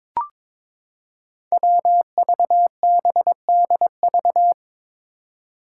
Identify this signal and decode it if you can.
Morse code, heard as a keyed tone, about 22 wpm, 700 Hz. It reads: WVBDV
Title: Five Challenging Letter-Number Combinations 22wpm